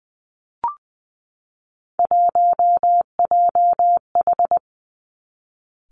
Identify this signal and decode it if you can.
Morse code, heard as a keyed tone, about 20 wpm, 700 Hz. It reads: 1JH
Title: Three Challenging Letter-Number Combinations 20wpm